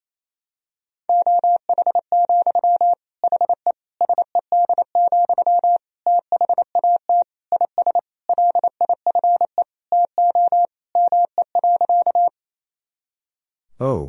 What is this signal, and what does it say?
Morse code, heard as a keyed tone, about 28 wpm, 700 Hz. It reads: OH, HE SED, THAT IS LIFE TO ME.
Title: Sentences from Top 500 Words and Abbreviations 28wpm